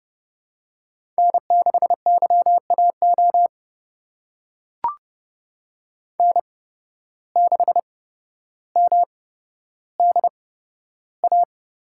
Morse code, heard as a keyed tone, about 30 wpm, spaced as for 13 wpm, 700 Hz. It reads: N6YAO N6MDA
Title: USA Call Signs - 5 Characters - 8x Character Spacing - 30wpm